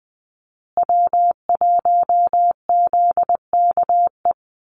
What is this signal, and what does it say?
Morse code, heard as a keyed tone, about 20 wpm, 700 Hz. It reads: W1ZKE
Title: USA Call Signs Repeated 3x 20wpm